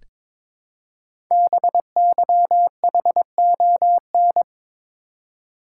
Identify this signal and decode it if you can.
Morse code, heard as a keyed tone, about 22 wpm, 700 Hz. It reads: BYHON
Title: Code Groups - 5 Letters - 22wpm